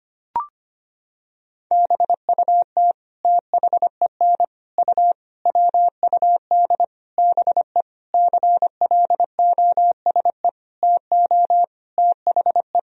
Morse code, heard as a keyed tone, about 25 wpm, 700 Hz. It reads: BUT THEN U WUD BE CLOSE TO THE
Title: Sentences from Top 300 Words and Abbreviations 25wpm